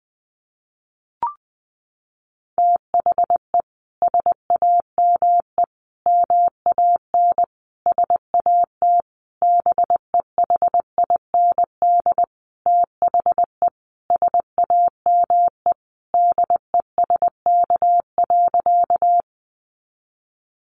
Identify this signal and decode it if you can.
Morse code, heard as a keyed tone, about 19 wpm, 700 Hz. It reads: THE SAME MAN SAT BEHIND THE SAME DESK.
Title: A Sound Of Thunder 20wpm